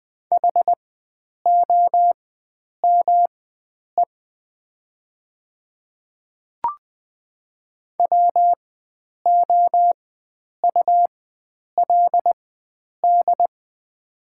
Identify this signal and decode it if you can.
Morse code, heard as a keyed tone, about 20 wpm, spaced as for 9 wpm, 700 Hz. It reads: HOME WOULD
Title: 4x Character Spacing - Top 300 Words - 20wpm